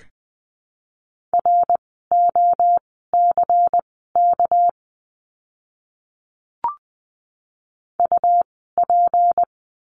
Morse code, heard as a keyed tone, about 19 wpm, spaced as for 14 wpm, 700 Hz. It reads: ROCK UP
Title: 199 Lesson 32f - Letter k - Words 20wpm